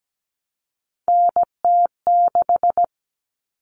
Morse code, heard as a keyed tone, about 17 wpm, 700 Hz. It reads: NT6
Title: All Cut Numbers - 3-Digit 17wpm